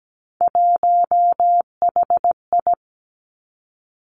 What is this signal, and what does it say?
Morse code, heard as a keyed tone, about 17 wpm, 700 Hz. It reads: 1HI
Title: Three Challenging Letter-Number Combinations 17wpm